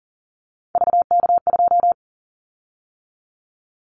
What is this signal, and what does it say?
Morse code, heard as a keyed tone, about 42 wpm, 700 Hz. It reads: VK2
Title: Callsign Prefixes for Top 100 DXCC - 40wpm